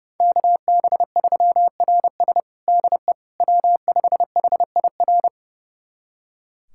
Morse code, heard as a keyed tone, about 30 wpm, 700 Hz. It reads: KB3RS DE W5HIR
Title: USA Callsign DE Callsign 30wpm